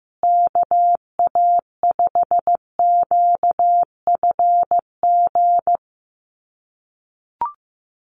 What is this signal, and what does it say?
Morse code, heard as a keyed tone, about 15 wpm, 700 Hz. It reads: KA5QFG
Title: USA Call Signs Repeated 3x 15wpm